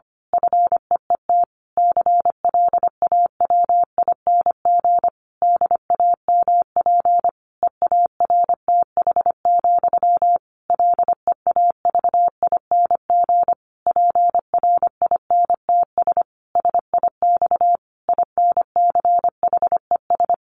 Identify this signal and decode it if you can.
Morse code, heard as a keyed tone, about 25 wpm, 700 Hz. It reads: FEET CLAWING DAMP EARTH, LEAVING PRINTS SIX INCHES
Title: A Sound Of Thunder 25wpm